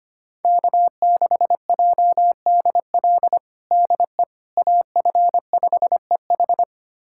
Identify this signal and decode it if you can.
Morse code, heard as a keyed tone, about 25 wpm, 700 Hz. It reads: K6JDL DE AF5EH
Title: USA Callsign DE Callsign - 5 Characters - 25wpm